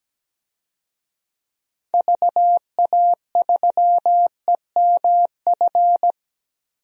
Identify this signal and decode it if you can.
Morse code, heard as a keyed tone, about 17 wpm, 700 Hz. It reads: VA3EMF
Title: DXCC-100 Call Signs - 6 Characters - 17wpm